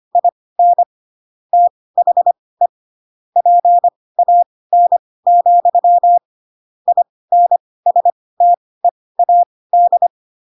Morse code, heard as a keyed tone, about 25 wpm, spaced as for 18 wpm, 700 Hz. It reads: IN THE PAN, INSTEAD